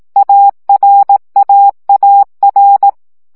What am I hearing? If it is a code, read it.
Morse code, heard as a keyed tone, about 18 wpm, 800 Hz. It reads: ARAAR
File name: CW gyakorlas a r w 15 baud